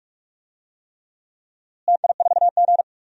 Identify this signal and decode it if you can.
Morse code, heard as a keyed tone, about 44 wpm, 700 Hz. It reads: TI4G
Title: DXCC-100 Call Signs - 4 Characters - Repeated 3x - 45wpm